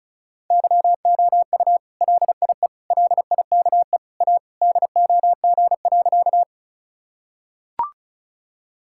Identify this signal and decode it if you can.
Morse code, heard as a keyed tone, about 36 wpm, 700 Hz. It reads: YOU LIE LIKE A DOG.
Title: Sentences from Top 1600 Words 35wpm